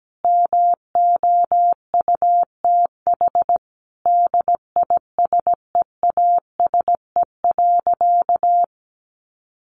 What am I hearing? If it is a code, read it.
Morse code, heard as a keyed tone, about 17 wpm, 700 Hz. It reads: MOUTH DISEASE.
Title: Sentences from Top 1500 Words 17wpm